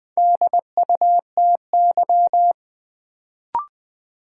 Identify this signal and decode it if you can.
Morse code, heard as a keyed tone, about 20 wpm, 700 Hz. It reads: DUTY
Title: Sentences from Top 1500 Words - New Word Drill - 20wpm